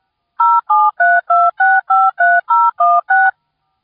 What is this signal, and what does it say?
DTMF, heard as telephone touch tones, tones of about 220 ms. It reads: 0732653016